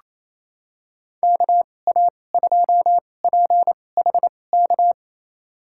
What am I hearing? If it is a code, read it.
Morse code, heard as a keyed tone, about 28 wpm, spaced as for 20 wpm, 700 Hz. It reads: KA2PHK
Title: USA Call Signs - 6 Characters - 2x Character Spacing - 28wpm